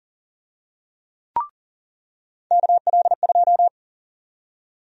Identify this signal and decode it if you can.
Morse code, heard as a keyed tone, about 40 wpm, 700 Hz. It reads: KL2
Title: USA Call Sign Prefixes - 40wpm